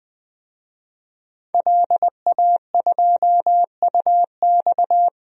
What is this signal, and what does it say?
Morse code, heard as a keyed tone, about 20 wpm, 700 Hz. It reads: LA2UX
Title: DXCC-100 Call Signs - 5 Characters - 20wpm